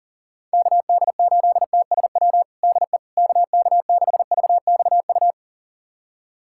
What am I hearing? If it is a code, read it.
Morse code, heard as a keyed tone, about 40 wpm, 700 Hz. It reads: KD8TSW DE KK6VXU
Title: USA Callsign DE Callsign - 6 Characters - 40wpm